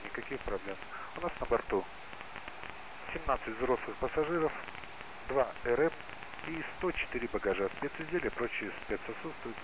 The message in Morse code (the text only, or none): none